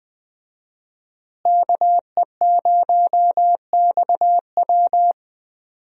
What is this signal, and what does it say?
Morse code, heard as a keyed tone, about 21 wpm, 700 Hz. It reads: KE0XW
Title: USA Call Signs - 5 Characters - Repeated 2x - 20wpm